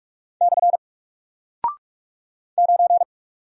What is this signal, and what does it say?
Morse code, heard as a keyed tone, about 44 wpm, 700 Hz. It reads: / 9
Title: Single Character (Letter:Number:Punctuation:Prosign) Alt 45wpm